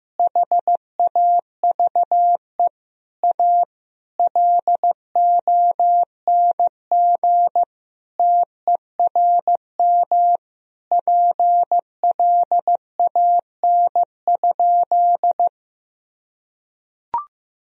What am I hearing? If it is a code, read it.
Morse code, heard as a keyed tone, about 15 wpm, 700 Hz. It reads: HAVE A LONG TERM PLAN?
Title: Sentences from Top 1000 Words - Review Part 4 - 15wpm